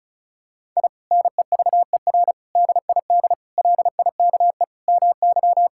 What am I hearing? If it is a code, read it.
Morse code, heard as a keyed tone, about 36 wpm, 700 Hz. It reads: I NEVER DID LIKE MY
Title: Sentences from Top 1000 Words - Review Part 5 - 35wpm